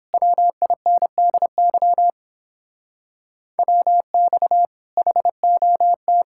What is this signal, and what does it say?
Morse code, heard as a keyed tone, about 29 wpm, 700 Hz. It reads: WINDY WX HOT